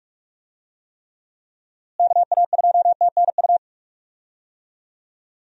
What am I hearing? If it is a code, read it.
Morse code, heard as a keyed tone, about 46 wpm, 700 Hz. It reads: KA2TNU
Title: USA Call Signs - 6 Characters - Repeated 2x - 45wpm